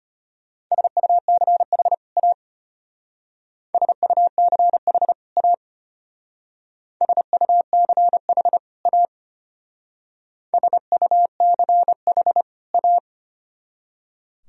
Morse code, heard as a keyed tone, about 32 wpm, 700 Hz. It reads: SUCH A SUCH A SUCH A SUCH A